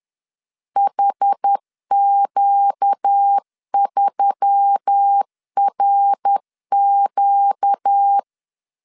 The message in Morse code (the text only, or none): HQ3RQ